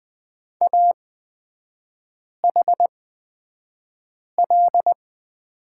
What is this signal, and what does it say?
Morse code, heard as a keyed tone, about 20 wpm, spaced as for 5 wpm, 690 Hz. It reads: AHL
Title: 229 Lesson 37e - Letter x - Three Characters 20wpm